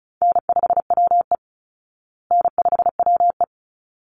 Morse code, heard as a keyed tone, about 34 wpm, 700 Hz. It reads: N5WE N5WE
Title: Easy Call Signs Repeated 2x - 35wpm